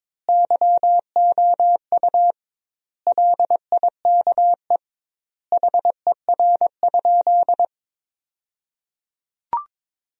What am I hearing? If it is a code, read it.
Morse code, heard as a keyed tone, about 22 wpm, 700 Hz. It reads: YOU LIKE HER?
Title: 2x Word-Spacing - Sentences from Top 100 Words - 22wpm